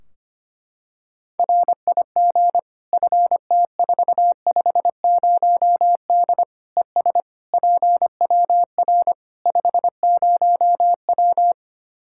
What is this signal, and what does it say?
Morse code, heard as a keyed tone, about 25 wpm, 700 Hz. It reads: RIG FT450D ES PWR 50W
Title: K1USN SST QSO Practice 25wpm